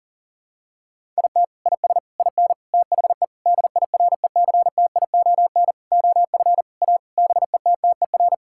Morse code, heard as a keyed tone, about 40 wpm, 700 Hz. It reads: IT IS IN THE DIRECTION OF A BETTER
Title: Sentences from Top 1000 Words - Review Part 5 - 40wpm